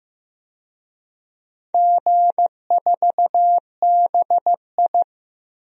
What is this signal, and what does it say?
Morse code, heard as a keyed tone, about 15 wpm, 700 Hz. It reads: G4BI
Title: DXCC-100 Call Signs - 4 Characters - Repeated 2x - 15wpm